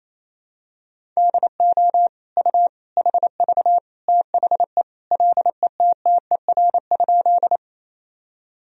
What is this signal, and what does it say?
Morse code, heard as a keyed tone, about 28 wpm, 700 Hz. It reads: DO U HV THE LETTER?
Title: Sentences from Top 300 Words and Abbreviations 28wpm